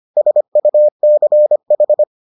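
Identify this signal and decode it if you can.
Morse code, heard as a keyed tone, about 25 wpm, 600 Hz. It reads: SUCH
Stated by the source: Morse edition of the Foundations of Amateur Radio episodes.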